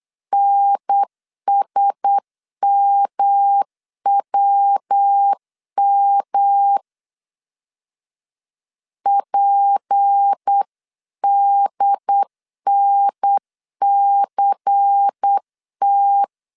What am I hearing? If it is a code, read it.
Morse code, heard as a keyed tone, about 8 wpm, 800 Hz. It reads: NSMWM PDNCT
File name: CW / DET / 1 6 1500C S B 8wpm 5142